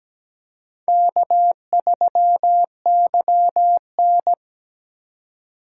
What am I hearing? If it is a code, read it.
Morse code, heard as a keyed tone, about 17 wpm, 700 Hz. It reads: K3YN
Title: USA Call Signs - 3x Character Spacing - 17wpm